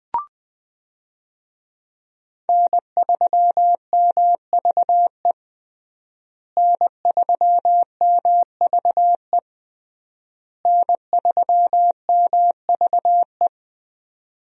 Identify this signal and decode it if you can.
Morse code, heard as a keyed tone, about 20 wpm, 700 Hz. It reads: N3MVE N3MVE N3MVE
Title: USA Call Signs - 5 Characters - Repeated 3x - 20wpm